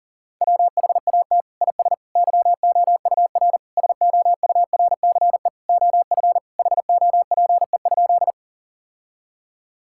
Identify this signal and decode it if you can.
Morse code, heard as a keyed tone, about 40 wpm, 700 Hz. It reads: WHAT IS YOUR SOURCE OF HOPE?
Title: Sentences from Top 1000 Words - Review Part 5 - 40wpm